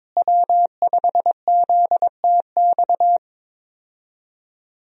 Morse code, heard as a keyed tone, about 21 wpm, 700 Hz. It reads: W5ZTX
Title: USA Call Signs - 5 Characters - Repeated 2x - 22wpm